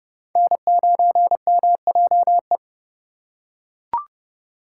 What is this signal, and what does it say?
Morse code, heard as a keyed tone, about 31 wpm, 700 Hz. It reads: N9MJE